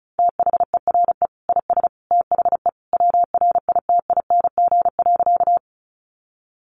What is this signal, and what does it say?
Morse code, heard as a keyed tone, about 34 wpm, 700 Hz. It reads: THERE IS THE WRITING.
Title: Sentences from Top 800 Words 35wpm